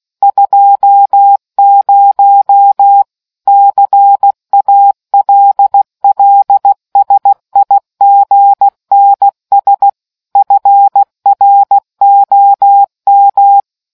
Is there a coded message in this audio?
Yes: Morse code from 0.2 to 13.6 s, heard as a keyed tone, about 16 wpm, 800 Hz. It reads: 20 CALLSIGNS FROM